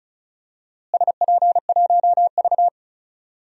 Morse code, heard as a keyed tone, about 36 wpm, 700 Hz. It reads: SP1V
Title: DXCC-100 Call Signs - 4 Characters - Repeated 3x - 35wpm